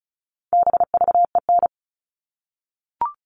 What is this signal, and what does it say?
Morse code, heard as a keyed tone, about 34 wpm, 700 Hz. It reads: BVEN
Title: All Cut Numbers - 4-Digit 35wpm